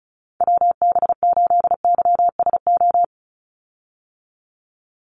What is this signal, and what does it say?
Morse code, heard as a keyed tone, about 36 wpm, 700 Hz. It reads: WB8YSO
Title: USA Call Signs 35wpm